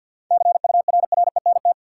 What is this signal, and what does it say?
Morse code, heard as a keyed tone, about 50 wpm, 700 Hz. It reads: CURRENT
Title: Sentences from Top 1000 Words - Review Part 4 - New Word Drill - 50wpm